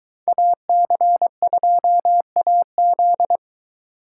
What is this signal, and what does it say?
Morse code, heard as a keyed tone, about 22 wpm, 700 Hz. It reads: AC2AZ